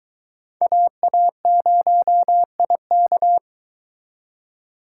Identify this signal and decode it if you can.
Morse code, heard as a keyed tone, about 23 wpm, 700 Hz. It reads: AA0IK